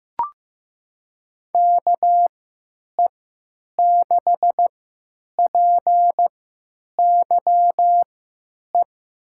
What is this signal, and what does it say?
Morse code, heard as a keyed tone, about 15 wpm, spaced as for 9 wpm, 700 Hz. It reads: KE6PYE
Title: USA Call Signs - 6 Characters - 3x Character Spacing - 15wpm